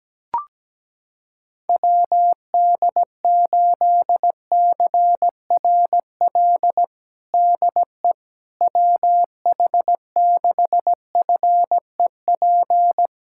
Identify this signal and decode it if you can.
Morse code, heard as a keyed tone, about 17 wpm, 700 Hz. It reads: WD8CRL DE WH6FEP
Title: USA Callsign DE Callsign - 6 Characters - 17wpm